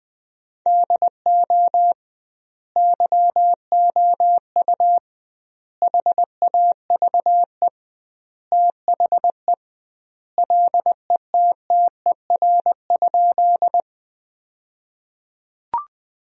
Morse code, heard as a keyed tone, about 20 wpm, 700 Hz. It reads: DO YOU HAVE THE LETTER?
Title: 2x Word-Spacing - Sentences from Top 300 Words - 20wpm